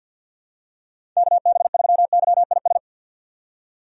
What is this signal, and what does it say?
Morse code, heard as a keyed tone, about 51 wpm, 700 Hz. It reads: KB3YIS
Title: USA Call Signs - 6 Characters - 50wpm